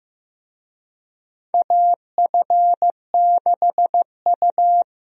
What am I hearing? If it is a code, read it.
Morse code, heard as a keyed tone, about 15 wpm, 700 Hz. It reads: AF6U
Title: USA Call Signs - 4 Characters - 15wpm